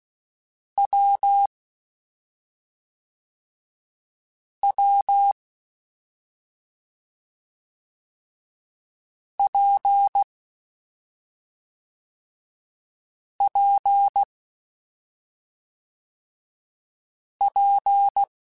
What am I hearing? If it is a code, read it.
Morse code, heard as a keyed tone, about 16 wpm, 800 Hz. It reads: W W P P P
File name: Difficult Combos / Difficult Combos-mp3s / WPJ WPJ1 @5